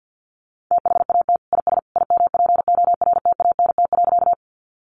Morse code, heard as a keyed tone, about 50 wpm, 700 Hz. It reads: THAT IS IMPORTANT.
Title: Sentences from Top 600 Words 50wpm